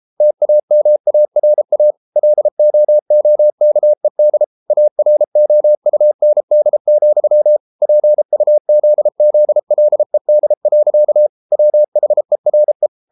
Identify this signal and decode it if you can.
Morse code, heard as a keyed tone, about 33 wpm, 600 Hz. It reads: TAMARA LOOKED AROUND, PUZZLED. WHERE